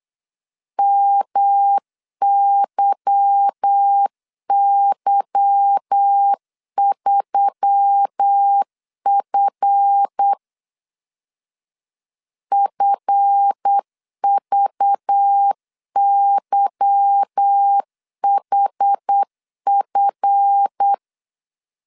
Morse code, heard as a keyed tone, about 8 wpm, 800 Hz. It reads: MYY3F FVYHF
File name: LEZIONI / 8 YJ45 / 8 Y 8wpm 3245